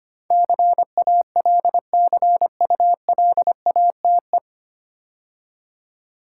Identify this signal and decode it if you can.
Morse code, heard as a keyed tone, about 25 wpm, 690 Hz. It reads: CALCULATE
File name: practice / 25wpm / 9 Letter Words - First 3 Letters Given - 25wpm